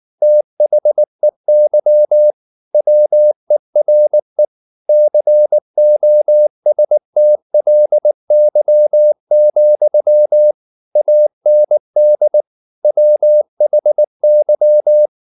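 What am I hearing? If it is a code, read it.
Morse code, heard as a keyed tone, about 19 wpm, 600 Hz. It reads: THEY WERE COSTLY, AND WHY
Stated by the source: Morse code at 19 wpm from CW med Gnister, reading Ken Follett's novel Never in English.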